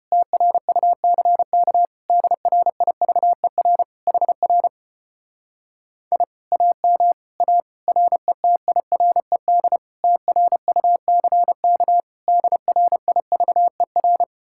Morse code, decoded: TRUCK DRIVER HR I AM A RETIRED TRUCK DRIVER

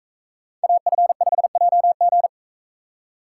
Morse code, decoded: AF5JG

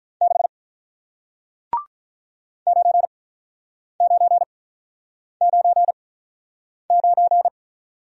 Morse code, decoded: 6 9999

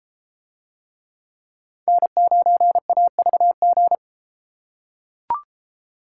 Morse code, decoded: N9AVG

